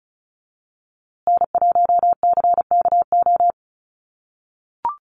Morse code, heard as a keyed tone, about 34 wpm, 700 Hz. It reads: N1CKO
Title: USA Call Signs Repeated 3x 35wpm